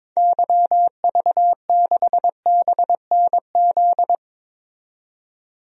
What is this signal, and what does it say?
Morse code, heard as a keyed tone, about 22 wpm, 700 Hz. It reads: YV6BNZ
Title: DXCC-100 Call Signs - 6 Characters - 22wpm